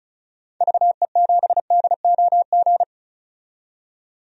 Morse code, heard as a keyed tone, about 36 wpm, 700 Hz. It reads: VE7DOG
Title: DXCC-100 Call Signs - 6 Characters - 35wpm